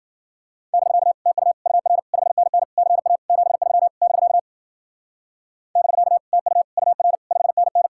Morse code, heard as a keyed tone, about 52 wpm, 700 Hz. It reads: <BK> TU UR 5NN CA 73 <BK> <BK> TU UR 5NN